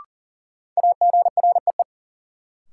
Morse code, heard as a keyed tone, about 40 wpm, 700 Hz. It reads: AGREE
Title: Sets of 5 Words Encore - New Word Drill - 40wpm